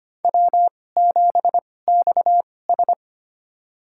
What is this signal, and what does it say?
Morse code, heard as a keyed tone, about 25 wpm, spaced as for 18 wpm, 700 Hz. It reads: W7XS
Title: USA Call Signs - 4 Characters - 2x Character Spacing - 25wpm